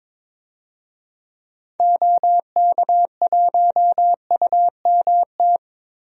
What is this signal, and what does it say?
Morse code, heard as a keyed tone, about 22 wpm, 700 Hz. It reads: OK1UMT
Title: DXCC-100 Call Signs - 6 Characters - Repeated 3x - 22wpm